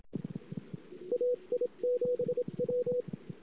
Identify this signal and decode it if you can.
Morse code, heard as a keyed tone, about 27 wpm, 490 Hz. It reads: AI7W
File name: CW / 2008-ss-cw-k5tr-op-n5rz -23:00